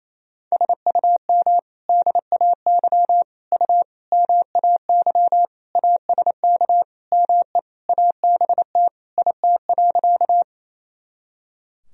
Morse code, heard as a keyed tone, about 28 wpm, 700 Hz. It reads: SUM DAY U MAY ASK ME ABT IT.
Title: Sentences from Top 200 Words and Abbreviations 28wpm